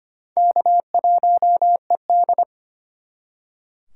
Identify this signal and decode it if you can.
Morse code, heard as a keyed tone, about 33 wpm, 700 Hz. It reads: K1ED